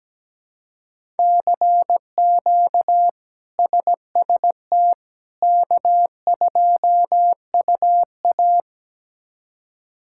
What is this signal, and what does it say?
Morse code, heard as a keyed tone, about 17 wpm, 700 Hz. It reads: CQ SST K2UA
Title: K1USN SST QSO Practice - 17wpm